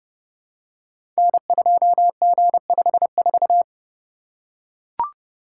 Morse code, heard as a keyed tone, about 29 wpm, 700 Hz. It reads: N2G54
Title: Code Groups - 5 Letter:Numbers - 30wpm